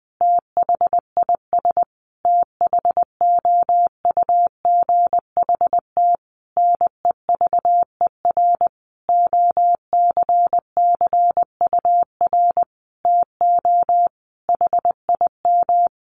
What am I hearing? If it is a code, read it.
Morse code, heard as a keyed tone, about 20 wpm, 700 Hz. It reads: THIS THOUGHT NEVER OCCUR TO HIM
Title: Sentences from Top 900 Words 20wpm